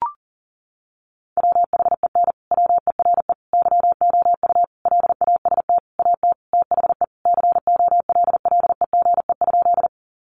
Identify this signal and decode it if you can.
Morse code, heard as a keyed tone, about 40 wpm, 700 Hz. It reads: WHEN WERE YOU LAST AT THE COLLEGE?
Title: Sentences from Top 1300 Words 40wpm